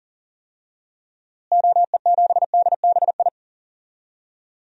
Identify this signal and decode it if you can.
Morse code, heard as a keyed tone, about 37 wpm, 700 Hz. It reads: OE7DBI
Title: DXCC-100 Call Signs - 6 Characters - Repeated 3x - 40wpm